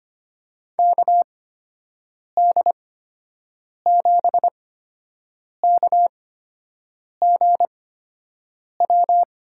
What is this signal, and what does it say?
Morse code, heard as a keyed tone, about 25 wpm, spaced as for 7 wpm, 700 Hz. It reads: KD7KGW
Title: USA Call Signs - 6 Characters - 8x Character Spacing - 25wpm